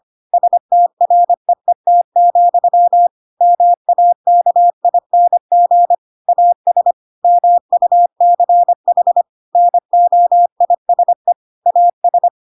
Morse code, heard as a keyed tone, about 25 wpm, 700 Hz. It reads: STREET, MAKING AS MUCH NOISE AS
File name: Pinocchio / Pinocchio ch3 25 25 ik1hge